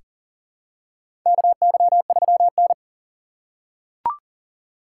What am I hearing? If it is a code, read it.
Morse code, heard as a keyed tone, about 39 wpm, 700 Hz. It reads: KY3N
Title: USA Call Signs - 6x Character Spacing - 40wpm